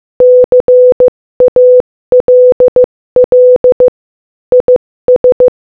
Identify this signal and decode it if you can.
Morse code, heard as a keyed tone, about 15 wpm, 500 Hz. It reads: CALL IS
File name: Morse-Code-Audio-Speed-15-15